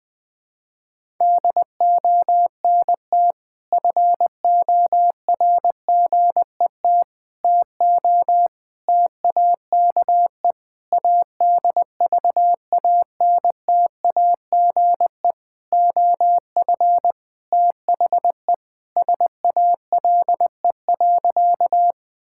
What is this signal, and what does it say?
Morse code, heard as a keyed tone, about 21 wpm, 700 Hz. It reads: DONT FORGET TO TAKE ADVANTAGE OF THE SALE.
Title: Sentences with 5-Word Phrases - 20wpm